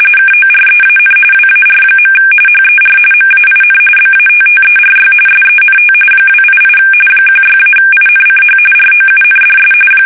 RTTY, about 45.45 baud. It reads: KOREAN WAR. HER MAIN BATTERY SENT 288 16-INCH ROUNDS
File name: fsk850 / battleshipnj